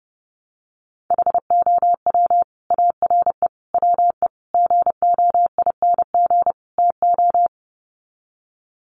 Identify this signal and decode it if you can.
Morse code, heard as a keyed tone, about 30 wpm, 700 Hz. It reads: HOW ARE WE GOING TO
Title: Sets of 5 Words Encore 30wpm